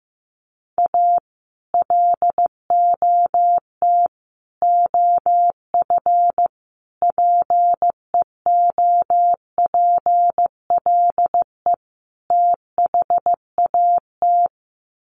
Morse code, heard as a keyed tone, about 15 wpm, 700 Hz. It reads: A LOT OF PEOPLE THAT
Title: Sets of 5 Words Encore2 15wpm